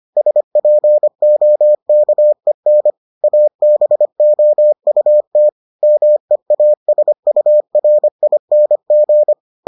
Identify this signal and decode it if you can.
Morse code, heard as a keyed tone, about 25 wpm, 600 Hz. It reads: SPOKEN ABOUT MEASURING